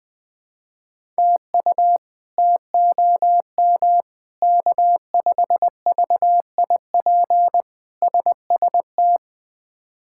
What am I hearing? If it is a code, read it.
Morse code, heard as a keyed tone, about 21 wpm, 700 Hz. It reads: TU TOM K5VIP SST
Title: K1USN SST QSO Practice - 20wpm